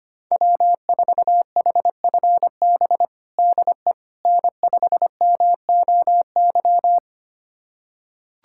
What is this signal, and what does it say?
Morse code, heard as a keyed tone, about 25 wpm, 700 Hz. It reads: W4HFB DE N5MOY
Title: USA Callsign DE Callsign - 5 Characters - 25wpm